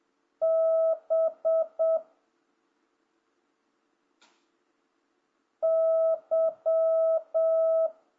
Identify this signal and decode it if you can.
Morse code, heard as a keyed tone, about 7 wpm, 1300 Hz. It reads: B Y